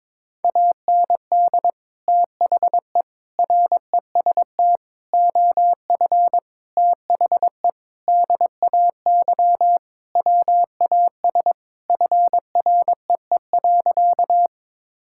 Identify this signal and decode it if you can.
Morse code, heard as a keyed tone, about 22 wpm, 700 Hz. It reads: AND THE REST OF THE DAY WAS FREE.
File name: practice / 22wpm / Sentences with 4-Word Phrases - 22wpm